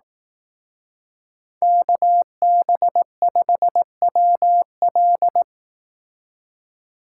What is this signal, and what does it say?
Morse code, heard as a keyed tone, about 18 wpm, 700 Hz. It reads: KB5WL
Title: Speed Racing - USA 2x2 Call Signs 15wpm